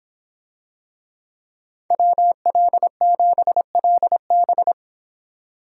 Morse code, heard as a keyed tone, about 26 wpm, 700 Hz. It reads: WL7LB